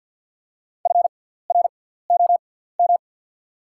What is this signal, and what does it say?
Morse code, heard as a keyed tone, about 48 wpm, spaced as for 17 wpm, 700 Hz. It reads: FROM
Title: 6x Character Spacing - Top 200 Words - 50wpm